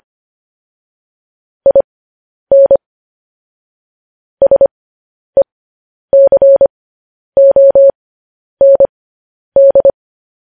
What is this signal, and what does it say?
Morse code, heard as a keyed tone, about 25 wpm, spaced as for 10 wpm, 550 Hz. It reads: IN SECOND